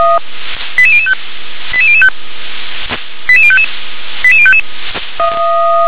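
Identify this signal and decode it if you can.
Five-tone selective call (ZVEI), heard as quick runs of five tones, tones of about 70 ms. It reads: 88005 88005 88055 88055
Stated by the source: ZVEI five-tone selective call 88060 received on an SDR.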